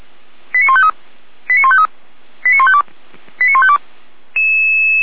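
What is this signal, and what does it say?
Five-tone selective call (ZVEI), heard as quick runs of five tones, tones of about 70 ms. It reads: 89153 89153 89152 89152
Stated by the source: Five-tone ZVEI selective call received on an SDR, from the feuerwehr-orlamuende.org fire brigade audio server.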